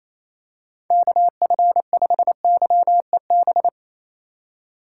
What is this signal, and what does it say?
Morse code, heard as a keyed tone, about 28 wpm, 700 Hz. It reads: KF5YEB
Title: USA Call Signs - 6 Characters - 4x Character Spacing - 28wpm